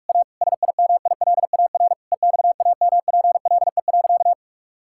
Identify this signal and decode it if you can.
Morse code, heard as a keyed tone, about 41 wpm, 690 Hz. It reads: A SIMILAR EXAMPLE.
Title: Sentences from Top 1000 Words - Review Part 5 - 45wpm